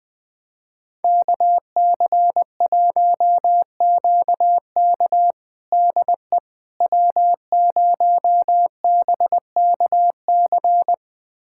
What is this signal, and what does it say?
Morse code, heard as a keyed tone, about 21 wpm, 700 Hz. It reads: KC1QK DE W0BKC
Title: USA Callsign DE Callsign - 5 Characters - 20wpm